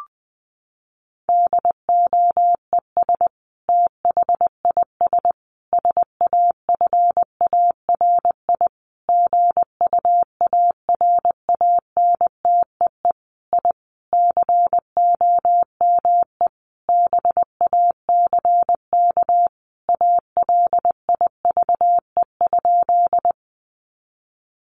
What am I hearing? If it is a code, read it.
Morse code, heard as a keyed tone, about 20 wpm, 700 Hz. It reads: DOES THIS SAFARI GUARANTEE I COME BACK ALIVE?
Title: A Sound Of Thunder 20wpm